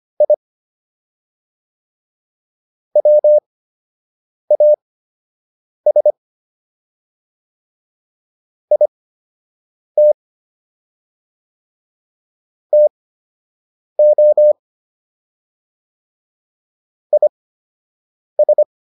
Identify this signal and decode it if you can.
Morse code, heard as a keyed tone, about 25 wpm, spaced as for 7 wpm, 600 Hz. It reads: I WAS IT TO IS